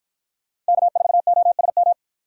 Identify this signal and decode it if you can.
Morse code, heard as a keyed tone, about 52 wpm, 700 Hz. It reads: K4OSM